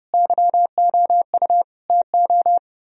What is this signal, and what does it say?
Morse code, heard as a keyed tone, about 33 wpm, 700 Hz. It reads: YOU TO